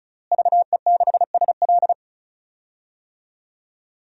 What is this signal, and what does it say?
Morse code, heard as a keyed tone, about 34 wpm, 700 Hz. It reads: VE6SL
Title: DXCC-100 Call Signs - 5 Characters - Repeated 2x - 35wpm